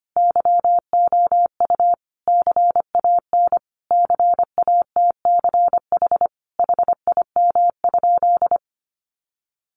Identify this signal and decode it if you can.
Morse code, heard as a keyed tone, about 25 wpm, 700 Hz. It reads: YOU CAN CATCH HIM?
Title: Sentences from Top 1500 Words 25wpm